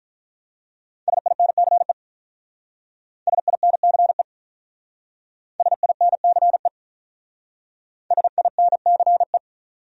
Morse code, heard as a keyed tone, about 45 wpm, 700 Hz. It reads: SINCE SINCE SINCE SINCE